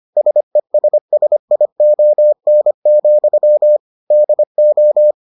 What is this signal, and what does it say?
Morse code, heard as a keyed tone, about 25 wpm, 600 Hz. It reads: SESSION, DO